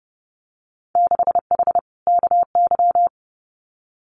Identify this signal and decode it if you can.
Morse code, heard as a keyed tone, about 30 wpm, 700 Hz. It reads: 6H KY
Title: Winter Field Day Exchanges Encore 30wpm